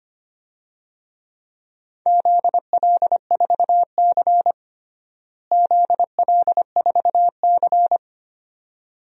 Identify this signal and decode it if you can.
Morse code, heard as a keyed tone, about 25 wpm, 700 Hz. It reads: ZL4C ZL4C
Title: DXCC-100 Call Signs - 4 Characters - Repeated 2x - 25wpm